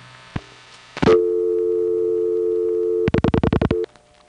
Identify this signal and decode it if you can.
Dial tone (North America), 350 + 440 Hz.